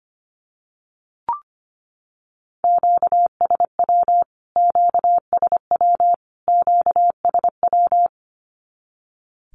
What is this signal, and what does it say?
Morse code, heard as a keyed tone, about 25 wpm, 700 Hz. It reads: QSW QSW QSW
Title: Q-Codes Repeated 3x 25wpm